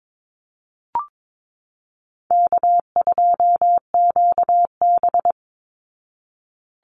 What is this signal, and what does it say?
Morse code, heard as a keyed tone, about 22 wpm, 700 Hz. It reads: K2QB
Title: Call Signs Repeated 3x - 22wpm